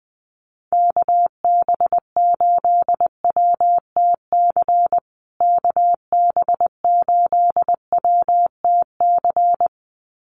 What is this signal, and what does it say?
Morse code, heard as a keyed tone, about 20 wpm, 700 Hz. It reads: KB8WTC KB8WTC
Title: USA Call Signs Repeated 2x 20wpm